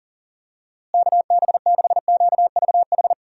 Morse code, heard as a keyed tone, about 40 wpm, 700 Hz. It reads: KB6QVH
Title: USA Call Signs - 6 Characters - Repeated 3x - 40wpm